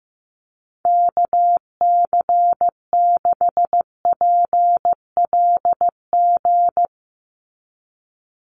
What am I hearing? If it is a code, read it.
Morse code, heard as a keyed tone, about 15 wpm, 700 Hz. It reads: KC6PLG
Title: USA Call Signs 15wpm